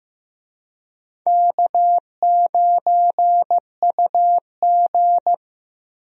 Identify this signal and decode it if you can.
Morse code, heard as a keyed tone, about 15 wpm, 700 Hz. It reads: K9UG
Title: USA Call Signs - 4 Characters - 15wpm